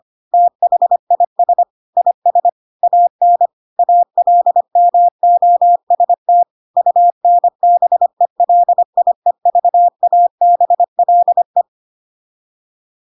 Morse code, heard as a keyed tone, about 25 wpm, 700 Hz. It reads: THIS IS AN ALMOST UNBELIEVABLE
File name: Pinocchio / Pinocchio ch3 25 25 ik1hge